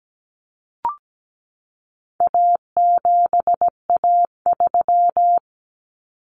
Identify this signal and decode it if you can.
Morse code, heard as a keyed tone, about 17 wpm, 700 Hz. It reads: A7A3
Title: Common Cut Numbers - 4-Digit 17wpm